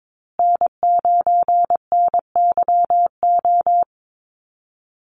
Morse code, heard as a keyed tone, about 22 wpm, 700 Hz. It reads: N9NYO
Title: USA Call Signs Repeated 2x 22wpm